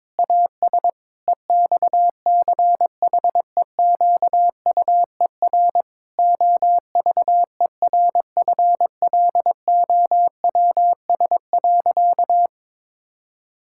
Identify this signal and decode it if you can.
Morse code, heard as a keyed tone, about 22 wpm, 700 Hz. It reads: AS EXCHEQUER OVERFLOWS.
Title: Pangram Challenge v2 - 22wpm